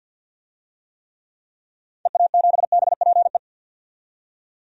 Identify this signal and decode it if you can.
Morse code, heard as a keyed tone, about 54 wpm, 700 Hz. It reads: EA7BPE
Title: DXCC-100 Call Signs - 6 Characters - 50wpm